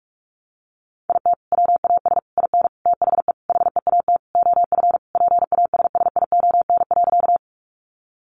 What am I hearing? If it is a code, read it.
Morse code, heard as a keyed tone, about 44 wpm, 700 Hz. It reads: IT WAS IN THE HEAT OF PASSION.
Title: Sentences from Top 1300 Words 45wpm